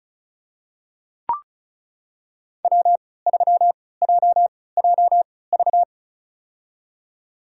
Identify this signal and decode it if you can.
Morse code, heard as a keyed tone, about 36 wpm, spaced as for 20 wpm, 690 Hz. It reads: W3JJV
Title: USA Call Signs - 5 Characters - 3x Character Spacing - 35wpm